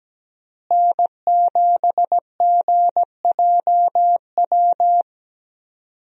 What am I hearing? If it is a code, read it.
Morse code, heard as a keyed tone, about 17 wpm, 700 Hz. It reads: N7GJW
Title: USA Call Signs - 5 Characters - Repeated 2x - 17wpm